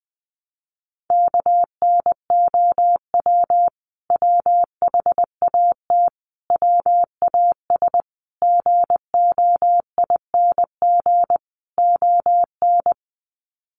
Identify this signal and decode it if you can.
Morse code, heard as a keyed tone, about 20 wpm, 700 Hz. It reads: KNOW WHAT WAS GOING ON
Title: Sets of 5 Words Encore2 20wpm